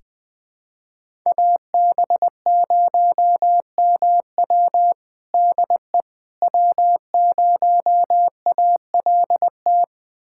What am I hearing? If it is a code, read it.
Morse code, heard as a keyed tone, about 21 wpm, 700 Hz. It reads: AB0MW DE W0ALT
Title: USA Callsign DE Callsign - 5 Characters - 20wpm